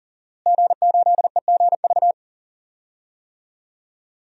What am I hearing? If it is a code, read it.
Morse code, heard as a keyed tone, about 40 wpm, 690 Hz. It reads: G8EGV
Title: DXCC-100 Call Signs - 5 Characters - Repeated 2x - 40wpm